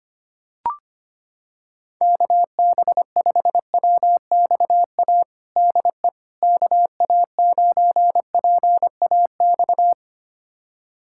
Morse code, heard as a keyed tone, about 25 wpm, 700 Hz. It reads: KB5WXA DE KA9PAX
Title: USA Callsign DE Callsign - 6 Characters - 25wpm